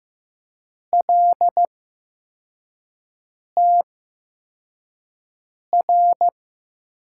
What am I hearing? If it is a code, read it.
Morse code, heard as a keyed tone, about 15 wpm, spaced as for 4 wpm, 700 Hz. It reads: LTR
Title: USA Call Signs - 6 Characters - 8x Character Spacing - 15wpm